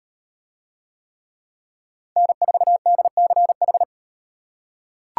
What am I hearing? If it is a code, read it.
Morse code, heard as a keyed tone, about 38 wpm, 700 Hz. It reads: N4DCH